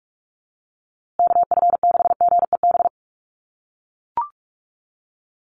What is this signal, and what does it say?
Morse code, heard as a keyed tone, about 44 wpm, 700 Hz. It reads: KF6GEB
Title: USA Call Signs Repeated 3x 45wpm